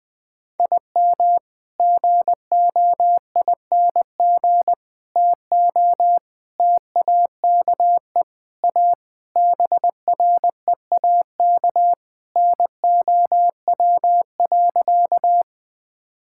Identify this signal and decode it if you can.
Morse code, heard as a keyed tone, about 19 wpm, 700 Hz. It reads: IM GOING TO TAKE A BREAK NOW.
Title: Sentences with 5-Word Phrases - 20wpm